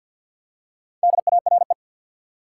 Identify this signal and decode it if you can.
Morse code, heard as a keyed tone, about 47 wpm, 700 Hz. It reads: DARE
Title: Four Letter Words - 50wpm